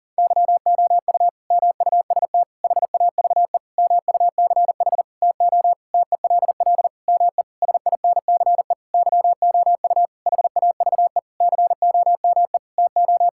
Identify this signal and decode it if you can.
Morse code, heard as a keyed tone, about 40 wpm, 700 Hz. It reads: YOU MUST HAVE MUCH TO TELL ME SINCE YOU HAVE COME TO
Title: Sentences from Top 1600 Words 40wpm